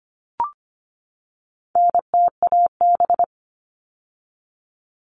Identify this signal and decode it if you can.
Morse code, heard as a keyed tone, about 24 wpm, 700 Hz. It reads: NTAB
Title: All Cut Numbers - 4-Digit 25wpm